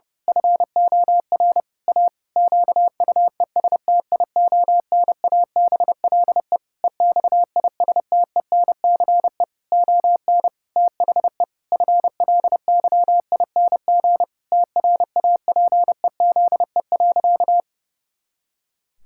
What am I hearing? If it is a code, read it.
Morse code, heard as a keyed tone, about 32 wpm, 700 Hz. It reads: FOR A QUESTIONABLE EXISTENCE ON THE FLYING TRAPEZE.
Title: Pangram Challenge v2 - 30wpm